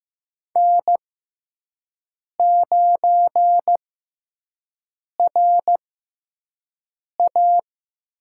Morse code, heard as a keyed tone, about 15 wpm, spaced as for 5 wpm, 700 Hz. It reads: N9RA
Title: USA Call Signs - 4 Characters - 6x Character Spacing - 15wpm